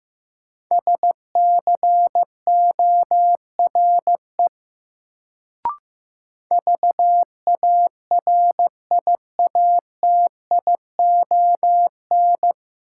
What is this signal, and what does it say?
Morse code, heard as a keyed tone, about 15 wpm, 700 Hz. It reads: SCORE VARIATION
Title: Sentences from Top 1600 Words - New Word Drill - 15wpm